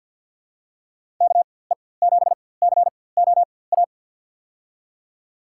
Morse code, heard as a keyed tone, about 53 wpm, spaced as for 24 wpm, 700 Hz. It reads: KE7COA
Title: USA Call Signs - 4x Character Spacing - 50wpm